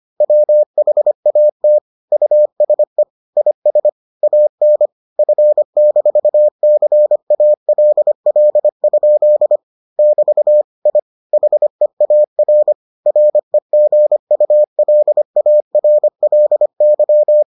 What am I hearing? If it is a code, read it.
Morse code, heard as a keyed tone, about 25 wpm, 600 Hz. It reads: WHAT USE IS AN F-CALL? <BT> I HEAR REGULARLY